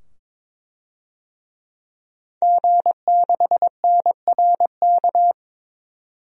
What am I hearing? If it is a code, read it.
Morse code, heard as a keyed tone, about 22 wpm, 700 Hz. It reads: G6NRK
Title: DXCC-100 Call Signs - 5 Characters - 22wpm